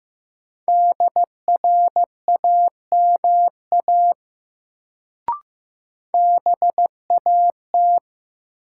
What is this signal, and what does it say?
Morse code, heard as a keyed tone, about 15 wpm, 700 Hz. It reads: DRAMA BAT
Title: Sentences from Top 1500 Words - New Word Drill Alt - 15wpm